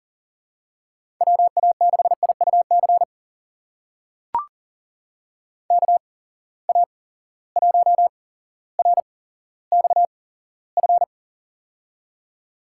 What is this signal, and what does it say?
Morse code, heard as a keyed tone, about 40 wpm, spaced as for 17 wpm, 700 Hz. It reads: WA6IUC KA1RXF
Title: USA Call Signs - 6 Characters - 8x Character Spacing - 40wpm